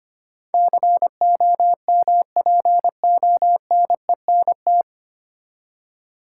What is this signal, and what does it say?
Morse code, heard as a keyed tone, about 25 wpm, 700 Hz. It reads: COMPONENT
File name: Nine Letter Words - 25wpm